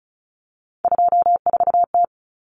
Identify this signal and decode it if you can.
Morse code, heard as a keyed tone, about 34 wpm, 700 Hz. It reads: 24T
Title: Common Cut Numbers - 3-Digit 35wpm